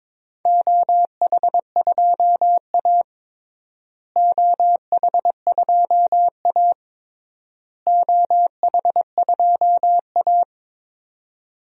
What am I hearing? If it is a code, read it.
Morse code, heard as a keyed tone, about 21 wpm, 700 Hz. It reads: OH2A OH2A OH2A
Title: DXCC-100 Call Signs - 4 Characters - Repeated 3x - 22wpm